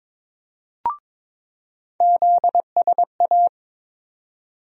Morse code, heard as a keyed tone, about 22 wpm, 700 Hz. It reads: ZSA
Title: USA Call Sign Postfixes - 3 Letters 22wpm